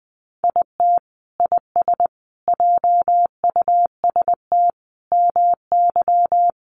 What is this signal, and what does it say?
Morse code, heard as a keyed tone, about 20 wpm, 700 Hz. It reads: IT IS JUST MY
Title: Sentences from Top 600 Words 20wpm